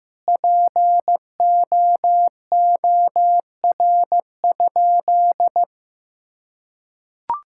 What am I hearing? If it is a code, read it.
Morse code, heard as a keyed tone, about 15 wpm, 700 Hz. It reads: POOR?
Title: Sentences from Top 1600 Words 15wpm